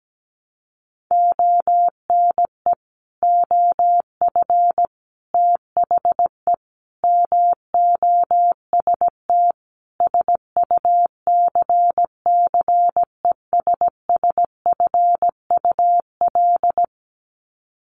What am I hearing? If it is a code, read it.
Morse code, heard as a keyed tone, about 17 wpm, 700 Hz. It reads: ONE OF THE MOST SUCCESSFUL
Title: Sets of 5 Words Encore2 17wpm